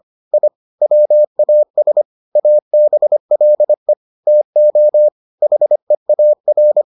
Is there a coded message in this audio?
Yes: Morse code from 0.3 to 6.8 s, heard as a keyed tone, about 25 wpm, 600 Hz. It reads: I WAS ABLE TO HEAR